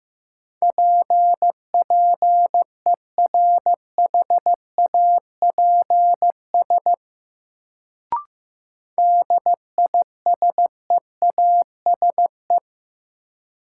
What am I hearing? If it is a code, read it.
Morse code, heard as a keyed tone, about 15 wpm, 700 Hz. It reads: PPERHAPS DISEASE
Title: Sentences from Top 1100 Words - New Word Drill Alt - 15wpm